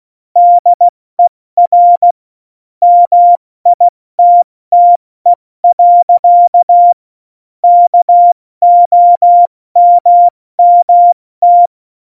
Morse code, heard as a keyed tone, about 16 wpm, 700 Hz. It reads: DER MITTE. KOMMT
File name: Morsekurs / Witze / Witz 23. April 2019, 70 F80